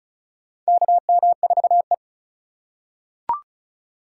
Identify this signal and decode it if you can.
Morse code, heard as a keyed tone, about 36 wpm, 700 Hz. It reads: KM4E
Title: USA Call Signs - 4 Characters - 4x Character Spacing - 35wpm